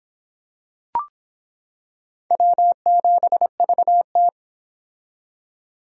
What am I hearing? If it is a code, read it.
Morse code, heard as a keyed tone, about 25 wpm, 700 Hz. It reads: W7VT